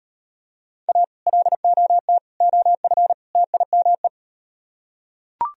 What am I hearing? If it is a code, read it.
Morse code, heard as a keyed tone, about 38 wpm, 700 Hz. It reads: A LOT OF TIME